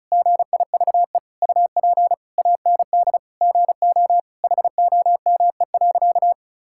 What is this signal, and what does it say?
Morse code, heard as a keyed tone, about 34 wpm, 700 Hz. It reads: GIVE UP AND GO HOME.
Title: Sentences from Top 1000 Words - Review Part 6 - 35wpm